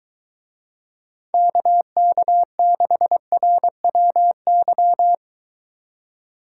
Morse code, heard as a keyed tone, about 23 wpm, 700 Hz. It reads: KK6RWY